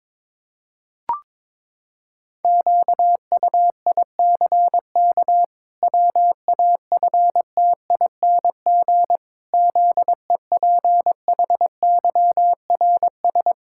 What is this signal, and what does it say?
Morse code, heard as a keyed tone, about 22 wpm, 700 Hz. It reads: QUICK WAFTING ZEPHYRS
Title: Pangram Challenge v2 - 22wpm